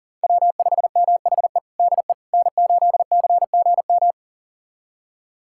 Morse code, heard as a keyed tone, about 40 wpm, 700 Hz. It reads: W5MHE DE N8CGM
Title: USA Callsign DE Callsign - 5 Characters - 40wpm